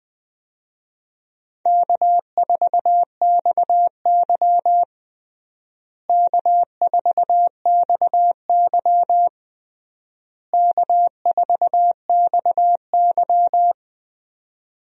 Morse code, heard as a keyed tone, about 20 wpm, 700 Hz. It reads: K4XY K4XY K4XY
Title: USA Call Signs - 4 Characters - Repeated 3x - 20wpm